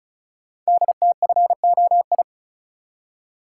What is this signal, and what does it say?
Morse code, heard as a keyed tone, about 36 wpm, 700 Hz. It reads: DTFOI
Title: Code Groups - 5 Letters - 35wpm